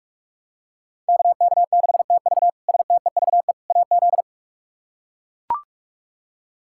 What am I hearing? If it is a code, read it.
Morse code, heard as a keyed tone, about 42 wpm, 700 Hz. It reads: KK6TV STEVE AZ
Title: K1USN SST QSO Practice - 45wpm